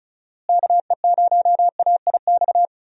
Morse code, heard as a keyed tone, about 36 wpm, 700 Hz. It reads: KE0AIX
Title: USA Call Signs - 6 Characters - Repeated 2x - 35wpm